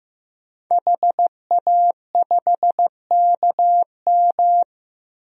Morse code, heard as a keyed tone, about 15 wpm, 700 Hz. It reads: HA5KM
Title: DXCC-100 Call Signs - 5 Characters - Repeated 2x - 15wpm